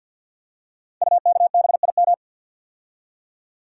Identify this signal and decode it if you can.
Morse code, heard as a keyed tone, about 48 wpm, 690 Hz. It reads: UKBIM
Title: Code Groups - 5 Letters - 50wpm